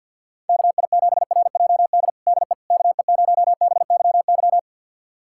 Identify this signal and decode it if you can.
Morse code, heard as a keyed tone, about 46 wpm, 700 Hz. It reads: KI7RJD DE KE0BYY
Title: USA Callsign DE Callsign - 6 Characters - 50wpm